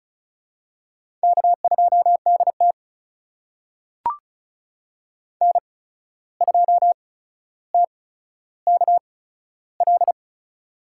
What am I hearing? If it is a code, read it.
Morse code, heard as a keyed tone, about 36 wpm, spaced as for 14 wpm, 700 Hz. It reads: K2DT N2TKL
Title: USA Call Signs - 8x Character Spacing - 35wpm